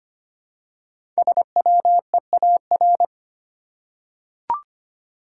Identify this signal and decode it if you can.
Morse code, heard as a keyed tone, about 25 wpm, 700 Hz. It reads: SWEAR
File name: 5 Letter Words - First 3 Letters Given - 25wpm